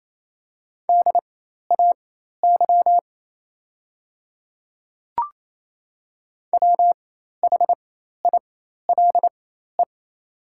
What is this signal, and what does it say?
Morse code, heard as a keyed tone, about 27 wpm, spaced as for 13 wpm, 700 Hz. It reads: DAY WHILE
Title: 4x Character Spacing - Top 300 Words - 28wpm